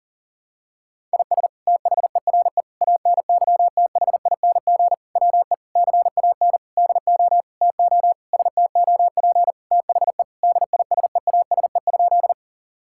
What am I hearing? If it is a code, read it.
Morse code, heard as a keyed tone, about 40 wpm, 700 Hz. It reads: IS THERE ANYTHING WE CAN DO TO STOP THE DISEASE?
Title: Sentences from Top 1000 Words - Review Part 6 - 40wpm